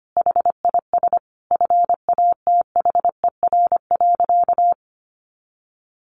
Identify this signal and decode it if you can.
Morse code, heard as a keyed tone, about 25 wpm, 700 Hz. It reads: HIS FATHER.
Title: Sentences from Top 1500 Words 25wpm